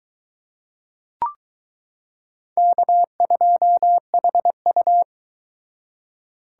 Morse code, heard as a keyed tone, about 22 wpm, 700 Hz. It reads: K2HU